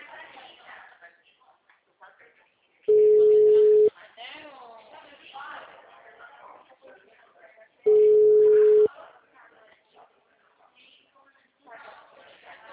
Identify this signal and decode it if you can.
Ringback tone (Europe, 425 Hz), with speech.